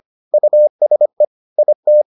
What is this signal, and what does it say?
Morse code, heard as a keyed tone, about 25 wpm, 600 Hz. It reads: USE IT